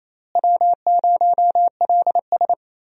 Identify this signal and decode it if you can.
Morse code, heard as a keyed tone, about 28 wpm, 700 Hz. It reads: W0LS
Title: USA Call Signs - 4 Characters - Repeated 2x - 28wpm